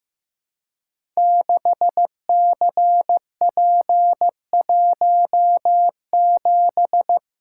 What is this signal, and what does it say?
Morse code, heard as a keyed tone, about 15 wpm, 700 Hz. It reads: 6CP17
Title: Code Groups - 5 Letter:Numbers - 15wpm